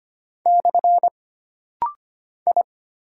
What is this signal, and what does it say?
Morse code, heard as a keyed tone, about 25 wpm, 700 Hz. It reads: / I
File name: Single Character (Letter:Number:Punctuation:Prosign) - Rapid-Fire - 25wpm